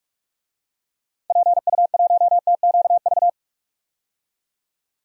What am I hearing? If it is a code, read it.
Morse code, heard as a keyed tone, about 42 wpm, 700 Hz. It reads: PU1TQV
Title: DXCC-100 Call Signs - 6 Characters - 45wpm